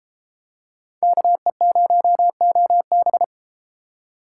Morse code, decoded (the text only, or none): KE0OB